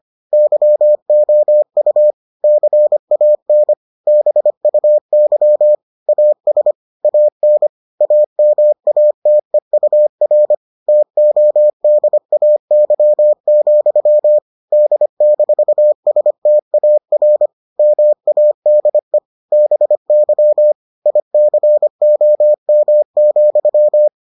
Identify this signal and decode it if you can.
Morse code, heard as a keyed tone, about 25 wpm, 600 Hz. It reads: YOU CAN BUY AS AN AMATEUR TODAY, D-STAR MADE BY ICOM,